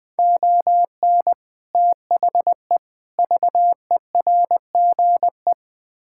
Morse code, heard as a keyed tone, about 20 wpm, 700 Hz. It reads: ON THE VERGE
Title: Build A Phrase - 4-Word Phrases - 20wpm